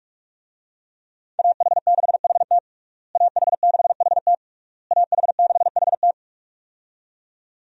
Morse code, heard as a keyed tone, about 41 wpm, 700 Hz. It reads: AH6HT AH6HT AH6HT
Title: USA Call Signs - 5 Characters - Repeated 3x - 45wpm